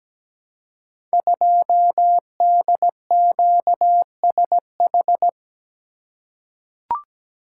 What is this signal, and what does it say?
Morse code, heard as a keyed tone, about 17 wpm, 700 Hz. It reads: 2DQSH
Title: Code Groups - 5 Letter:Numbers - 17wpm